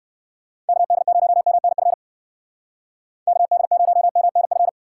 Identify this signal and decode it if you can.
Morse code, heard as a keyed tone, about 59 wpm, 700 Hz. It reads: KD0MNF KD0MNF